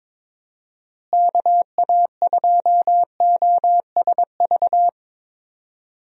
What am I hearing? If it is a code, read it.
Morse code, heard as a keyed tone, about 22 wpm, 700 Hz. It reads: KA2OSV
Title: USA Call Signs - 6 Characters - Repeated 2x - 22wpm